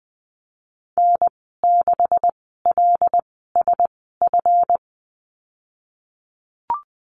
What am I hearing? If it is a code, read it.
Morse code, heard as a keyed tone, about 20 wpm, spaced as for 15 wpm, 700 Hz. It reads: N6LSF
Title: 161 Lesson 26g - Letter g - Call Signs 20wpm